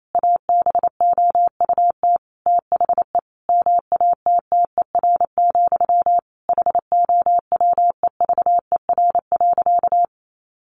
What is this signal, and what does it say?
Morse code, heard as a keyed tone, about 28 wpm, 700 Hz. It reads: ABOUT THE MATTER, HOWEVER.
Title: Sentences from Top 700 Words 28wpm